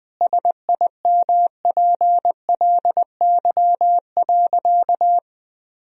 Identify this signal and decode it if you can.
Morse code, heard as a keyed tone, about 21 wpm, 700 Hz. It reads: SIMPLY.
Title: Sentences from Top 1000 Words - Review Part 4 - 20wpm